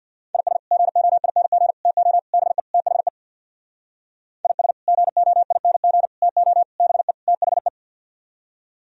Morse code, heard as a keyed tone, about 61 wpm, 700 Hz. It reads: IS GOING TO BE THE IS GOING TO BE THE
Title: Speed-Racing - Sets of 5 Words 50wpm